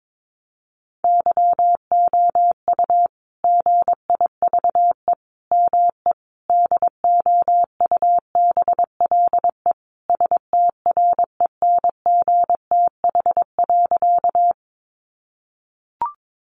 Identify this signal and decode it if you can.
Morse code, heard as a keyed tone, about 22 wpm, 700 Hz. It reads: YOU GIVE ME DOUBLE STRENGTH.
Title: Sentences from Top 1400 Words 22wpm